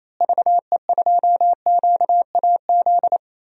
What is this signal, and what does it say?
Morse code, heard as a keyed tone, about 28 wpm, 700 Hz. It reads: VE2QAZ
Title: DXCC-100 Call Signs - 6 Characters - Repeated 2x - 28wpm